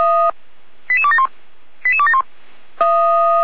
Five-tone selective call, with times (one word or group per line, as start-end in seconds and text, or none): 0.9-1.3 88271
1.9-2.2 88271